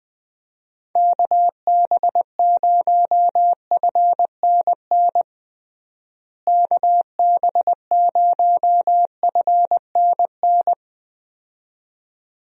Morse code, heard as a keyed tone, about 21 wpm, 700 Hz. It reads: KB0FNN KB0FNN
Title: USA Call Signs - 6 Characters - Repeated 2x - 20wpm